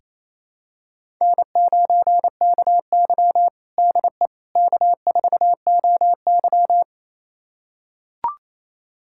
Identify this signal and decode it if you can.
Morse code, heard as a keyed tone, about 28 wpm, 700 Hz. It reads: N9KY DE K4OY
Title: USA Callsign DE Callsign - 4 Characters - 28wpm